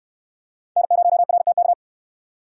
Morse code, heard as a keyed tone, about 61 wpm, 700 Hz. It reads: N0RTK